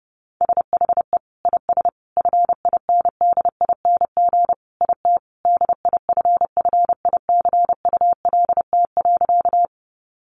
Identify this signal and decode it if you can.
Morse code, heard as a keyed tone, about 28 wpm, 700 Hz. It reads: SHE IS FINDING IT DIFFICULT.
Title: Sentences from Top 1100 Words 30wpm